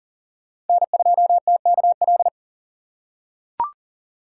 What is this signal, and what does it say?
Morse code, heard as a keyed tone, about 40 wpm, 700 Hz. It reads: N2TKL
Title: USA Call Signs - 4x Character Spacing - 40wpm